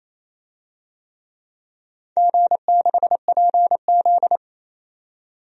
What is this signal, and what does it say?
Morse code, heard as a keyed tone, about 27 wpm, 700 Hz. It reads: G6PZ
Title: DXCC-100 Call Signs - 4 Characters - 28wpm